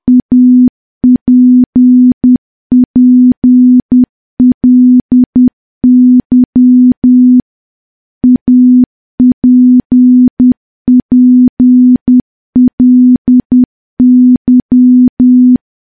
Morse code, heard as a keyed tone, about 10 wpm, 250 Hz. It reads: APPLY APPLY
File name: morse / morse-training-10WPM